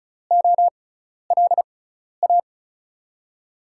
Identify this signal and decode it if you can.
Morse code, heard as a keyed tone, about 36 wpm, spaced as for 12 wpm, 700 Hz. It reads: OLA
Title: USA Call Signs - 6 Characters - 6x Character Spacing - 35wpm